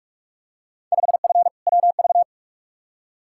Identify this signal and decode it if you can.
Morse code, heard as a keyed tone, about 42 wpm, 700 Hz. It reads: 5F WV
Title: Field Day Exchanges Encore - 45wpm